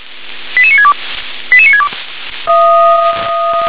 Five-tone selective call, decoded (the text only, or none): 88962 88962